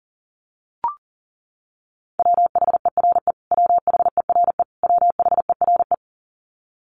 Morse code, heard as a keyed tone, about 40 wpm, 700 Hz. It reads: WHERE WHERE WHERE
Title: Top 500 Words Repeated 40wpm